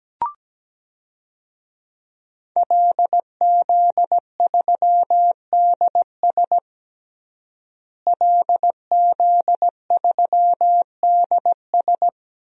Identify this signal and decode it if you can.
Morse code, heard as a keyed tone, about 17 wpm, 700 Hz. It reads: LZ3DS LZ3DS
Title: DXCC-100 Call Signs - 5 Characters - Repeated 2x - 17wpm